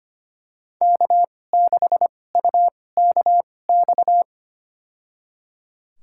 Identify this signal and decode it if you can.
Morse code, heard as a keyed tone, about 25 wpm, spaced as for 18 wpm, 700 Hz. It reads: K6UKX
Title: USA Call Signs - 5 Characters - 2x Character Spacing - 25wpm